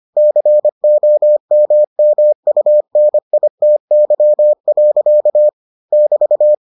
Morse code, heard as a keyed tone, about 25 wpm, 600 Hz. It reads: COMMUNITY. <BT>